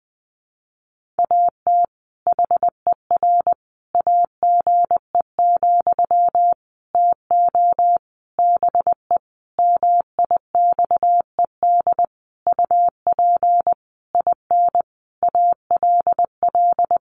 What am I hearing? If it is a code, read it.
Morse code, heard as a keyed tone, about 20 wpm, 700 Hz. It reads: AT HER AGE, TO BE MIXED UP IN ALL
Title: Sentences from Top 1500 Words 20wpm